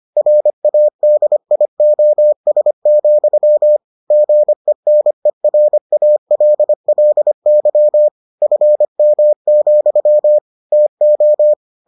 Morse code, heard as a keyed tone, about 25 wpm, 600 Hz. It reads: RADIOS, GENERALLY FM, TO